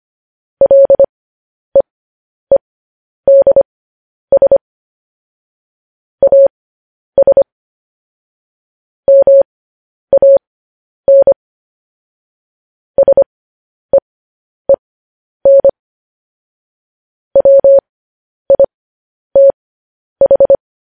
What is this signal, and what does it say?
Morse code, heard as a keyed tone, about 25 wpm, spaced as for 10 wpm, 550 Hz. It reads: LEEDS AS MAN SEEN WITH
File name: podcast / News headlines-1x-25x10